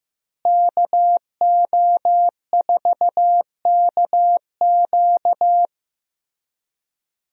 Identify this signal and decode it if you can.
Morse code, heard as a keyed tone, about 15 wpm, 700 Hz. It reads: KO4KQ
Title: USA Call Signs - 5 Characters - Repeated 3x - 15wpm